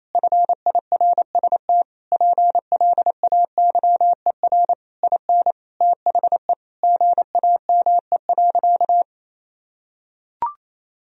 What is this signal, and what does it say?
Morse code, heard as a keyed tone, about 28 wpm, 700 Hz. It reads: FIRST PLAYER IN THE GAME.
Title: Sentences from Top 1000 Words - Review Part 4 - 28wpm